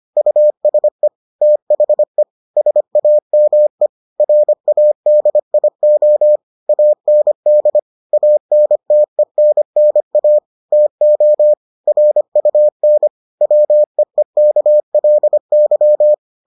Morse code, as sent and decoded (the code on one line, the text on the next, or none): ..- ... / - .... / ... .- -- . / .-. .- -.. .. --- / .- -. -.. / .- -. - . -. -. .- / - --- / .-. ..- -. / .-- . -.- .-.. -.--
USE THE SAME RADIO AND ANTENNA TO RUN WEEKLY